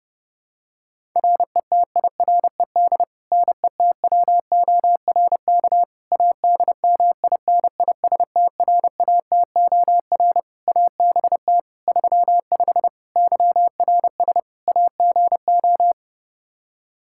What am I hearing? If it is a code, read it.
Morse code, heard as a keyed tone, about 32 wpm, 700 Hz. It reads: RETIRED NETWORK ADMINISTRATOR ABT 35 YRS AGO
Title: Practice QSO - Exchange 3 - More Info - 30wpm